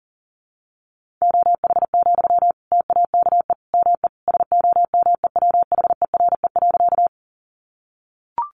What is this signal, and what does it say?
Morse code, heard as a keyed tone, about 40 wpm, 700 Hz. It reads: OH, TAKE ME SOMEWHERE.
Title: Sentences from Top 1400 Words 40wpm